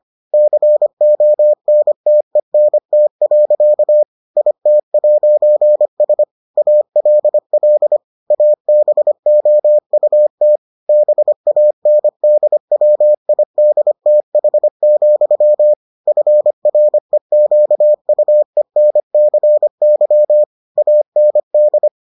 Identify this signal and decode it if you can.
Morse code, heard as a keyed tone, about 25 wpm, 600 Hz. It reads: CONTENT. IT'S ALL ABOUT BANDWIDTH, FREQUENCY AND